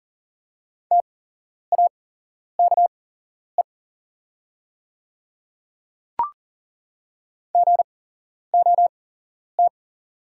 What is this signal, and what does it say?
Morse code, heard as a keyed tone, about 40 wpm, spaced as for 11 wpm, 700 Hz. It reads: TAKE GOT
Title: 8x Character Spacing - Top 300 Words - 40wpm